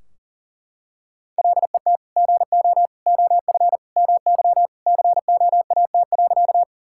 Morse code, heard as a keyed tone, about 37 wpm, 700 Hz. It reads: LET GO OF MY COAT.
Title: Sentences from Top 1600 Words 40wpm